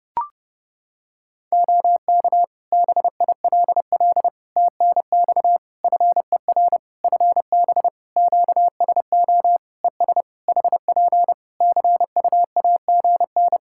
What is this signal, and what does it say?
Morse code, heard as a keyed tone, about 30 wpm, 700 Hz. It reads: OK BILL TNX FER FB QSO ES HP CUAGN
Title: Practice QSO - Exchange 4 - Ending - 30wpm